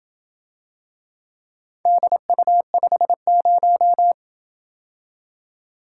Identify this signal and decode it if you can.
Morse code, heard as a keyed tone, about 28 wpm, 700 Hz. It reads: DU50